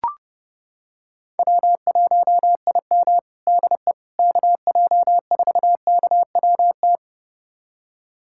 Morse code, heard as a keyed tone, about 30 wpm, 700 Hz. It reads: W1IM DE KJ4KWT
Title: USA Callsign DE Callsign 30wpm